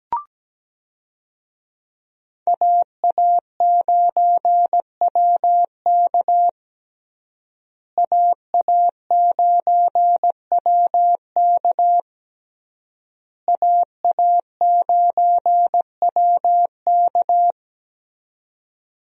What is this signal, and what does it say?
Morse code, heard as a keyed tone, about 17 wpm, 700 Hz. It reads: AA9WK AA9WK AA9WK
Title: USA Call Signs - 5 Characters - Repeated 3x - 17wpm